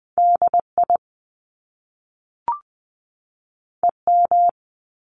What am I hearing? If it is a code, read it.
Morse code, heard as a keyed tone, about 20 wpm, 700 Hz. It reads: DI EM
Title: Top Two Letter Combinations in Words 20wpm